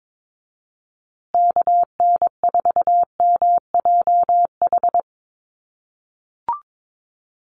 Morse code, heard as a keyed tone, about 22 wpm, 700 Hz. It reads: KN4MJH
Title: USA Call Signs Repeated 2x 22wpm